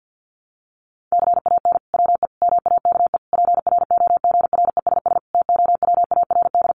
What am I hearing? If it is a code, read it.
Morse code, heard as a keyed tone, about 50 wpm, 700 Hz. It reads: CAN WE MAKE PROGRESS TOWARD
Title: Sentences from Top 1500 Words 50wpm